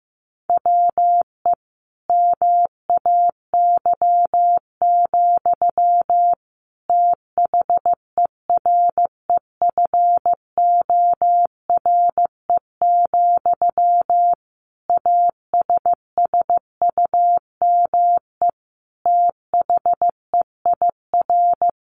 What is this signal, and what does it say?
Morse code, heard as a keyed tone, about 15 wpm, 700 Hz. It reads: WE MAY, THEREFORE, ASSUME THEIR
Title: Sentences from Top 1500 Words 15wpm